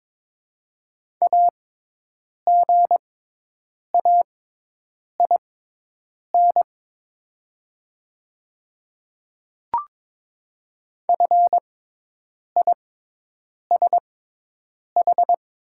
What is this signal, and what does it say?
Morse code, heard as a keyed tone, about 22 wpm, spaced as for 8 wpm, 700 Hz. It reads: AGAIN FISH
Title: 6x Character Spacing - Top 300 Words - 22wpm